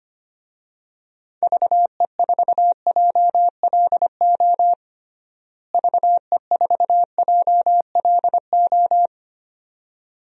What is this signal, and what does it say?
Morse code, heard as a keyed tone, about 25 wpm, 700 Hz. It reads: VE4JLO VE4JLO
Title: DXCC-100 Call Signs - 6 Characters - Repeated 2x - 25wpm